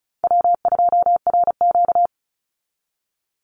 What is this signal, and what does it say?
Morse code, heard as a keyed tone, about 34 wpm, 700 Hz. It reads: W2RQ
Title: Call Signs Repeated 2x - 35wpm